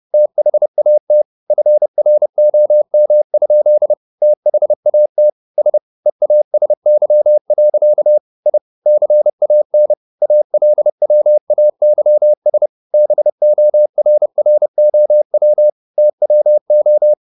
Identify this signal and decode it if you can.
Morse code, heard as a keyed tone, about 30 wpm, 600 Hz. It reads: THAT FROM? THAT S EASY. I CAN ALWAYS BORROW TWO
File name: the old man and the sea / the old man and the sea 1 30wpm